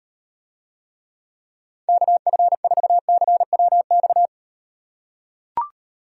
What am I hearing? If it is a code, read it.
Morse code, heard as a keyed tone, about 38 wpm, 700 Hz. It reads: KF4CWX